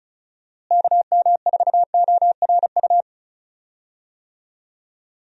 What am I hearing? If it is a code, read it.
Morse code, heard as a keyed tone, about 36 wpm, 700 Hz. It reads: KM4ORU
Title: USA Call Signs - 6 Characters - Repeated 2x - 35wpm